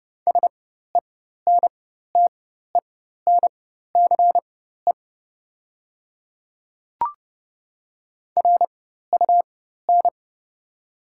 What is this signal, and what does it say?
Morse code, heard as a keyed tone, about 32 wpm, spaced as for 14 wpm, 700 Hz. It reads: SENTENCE RUN
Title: 4x Character Spacing - Top 300 Words - 30wpm